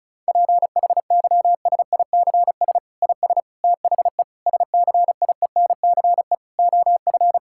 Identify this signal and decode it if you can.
Morse code, heard as a keyed tone, about 36 wpm, 700 Hz. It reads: PHYSICS IS THE SCIENCE OF
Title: Sentences from Top 1000 Words - Review Part 5 - 35wpm